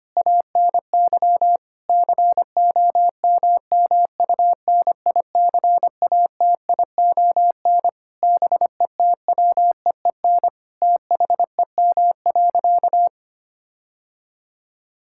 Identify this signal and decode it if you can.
Morse code, heard as a keyed tone, about 25 wpm, 700 Hz. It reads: ANY COMMUNICATION BETWEEN THEM.
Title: Sentences from Top 1000 Words - Review Part 6 - 25wpm